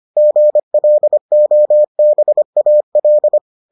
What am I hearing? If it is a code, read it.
Morse code, heard as a keyed tone, about 25 wpm, 600 Hz. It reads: GLOBAL